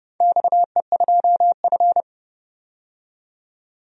Morse code, heard as a keyed tone, about 30 wpm, 700 Hz. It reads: XE2F
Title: DXCC-100 Call Signs - 4 Characters - 30wpm